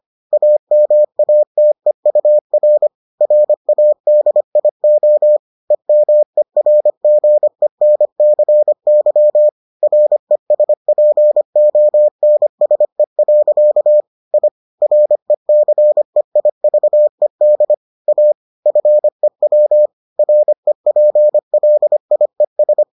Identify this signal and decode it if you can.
Morse code, heard as a keyed tone, about 25 wpm, 600 Hz. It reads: AMATEUR RADIO EMERGENCY RESPONSE. I RECEIVED A FEW REPLIES